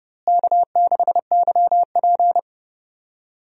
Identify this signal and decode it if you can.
Morse code, heard as a keyed tone, about 31 wpm, 700 Hz. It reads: K6YP
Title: USA Call Signs - 4 Characters - Repeated 2x - 30wpm